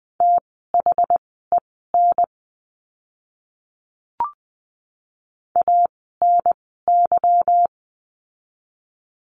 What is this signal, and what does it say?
Morse code, heard as a keyed tone, about 20 wpm, spaced as for 14 wpm, 700 Hz. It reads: THEN ANY
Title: 230 Lesson 37f - Letter x - Words 20wpm